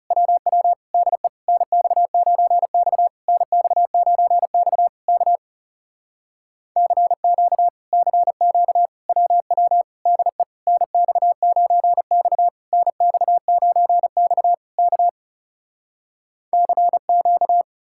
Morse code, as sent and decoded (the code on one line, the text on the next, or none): .-- .-- / -.. / -. -..- ----. -..- / -. -..- ----. -..- / -.- / -.-. --.- / -.-. --.- / .-- .-- / -.. / -. -..- ----. -..- / -. -..- ----. -..- / -.- / -.-. --.-
WW DE NX9X NX9X K CQ CQ WW DE NX9X NX9X K CQ